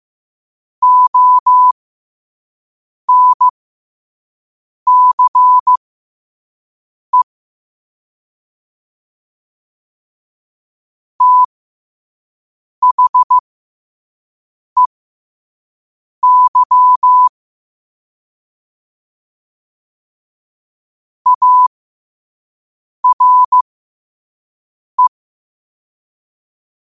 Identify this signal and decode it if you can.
Morse code, heard as a keyed tone, about 15 wpm, spaced as for 5 wpm, 1000 Hz. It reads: ONCE THEY ARE
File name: qotdfiles / 5wpm